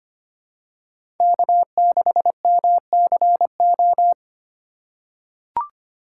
Morse code, heard as a keyed tone, about 25 wpm, 700 Hz. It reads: K6MCO
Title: USA Call Signs - 5 Characters - 2x Character Spacing - 25wpm